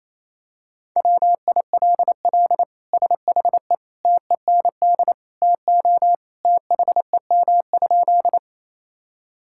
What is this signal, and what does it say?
Morse code, heard as a keyed tone, about 28 wpm, 700 Hz. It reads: WILL SHE TEND TO THEM?
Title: Sentences from Top 1000 Words - Review Part 5 - 28wpm